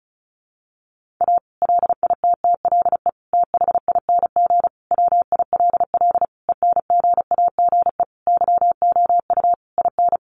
Morse code, heard as a keyed tone, about 34 wpm, 700 Hz. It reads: A LITTLE THING WILL ENGAGE YOU IN
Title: Sentences from Top 1500 Words 35wpm